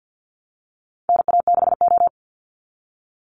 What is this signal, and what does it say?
Morse code, heard as a keyed tone, about 50 wpm, 700 Hz. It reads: NA6O
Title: Call Signs - 50wpm